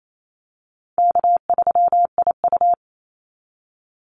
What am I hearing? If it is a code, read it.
Morse code, heard as a keyed tone, about 28 wpm, 700 Hz. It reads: K3IU
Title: Call Signs - 28wpm